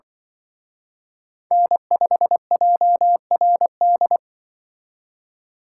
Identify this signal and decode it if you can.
Morse code, heard as a keyed tone, about 24 wpm, 700 Hz. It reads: N5JRD
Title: Speed Racing - USA 1x3 Call Signs 20wpm